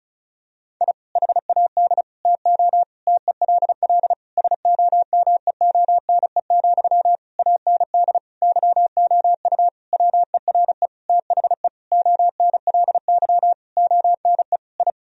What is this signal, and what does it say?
Morse code, heard as a keyed tone, about 34 wpm, 700 Hz. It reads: I HAD TO TELL SOMEONE, AND YOU WERE THE ONLY ONE I
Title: Sentences from Top 1000 Words - Review Part 4 - 35wpm